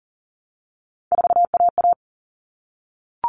Morse code, heard as a keyed tone, about 40 wpm, 700 Hz. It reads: 4AA
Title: All Cut Numbers - 3-Digit 40wpm